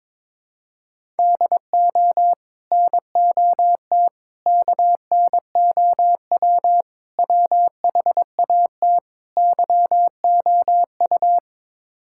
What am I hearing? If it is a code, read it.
Morse code, heard as a keyed tone, about 21 wpm, 700 Hz. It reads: DO NOT KNOW WHAT YOU
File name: practice / 22wpm / Build A Phrase - 5-Word Phrases - 22wpm